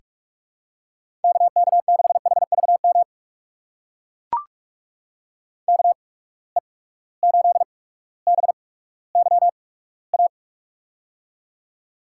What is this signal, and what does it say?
Morse code, heard as a keyed tone, about 44 wpm, spaced as for 19 wpm, 700 Hz. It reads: KK6HVM KE8BYA
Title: USA Call Signs - 6 Characters - 8x Character Spacing - 45wpm